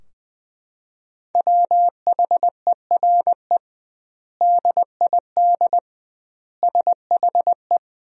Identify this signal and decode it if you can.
Morse code, heard as a keyed tone, about 20 wpm, 700 Hz. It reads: WHERE DID SHE
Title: 2x Word-Spacing - Sentences from Top 200 Words - 20wpm